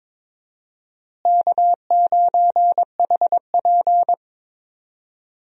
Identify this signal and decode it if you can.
Morse code, heard as a keyed tone, about 21 wpm, 700 Hz. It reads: K9HP
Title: USA Call Signs - 4 Characters - Repeated 2x - 22wpm